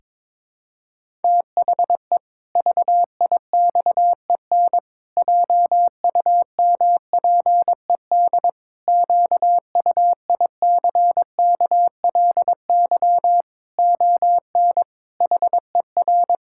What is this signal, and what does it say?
Morse code, heard as a keyed tone, about 22 wpm, 700 Hz. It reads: THE VIXEN JUMPED QUICKLY ON HER
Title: Pangram Challenge v2 - 22wpm